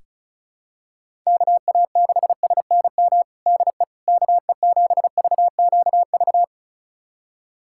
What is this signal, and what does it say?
Morse code, heard as a keyed tone, about 34 wpm, 700 Hz. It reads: KA6SNM DE KE7VQV
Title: USA Callsign DE Callsign - 6 Characters - 35wpm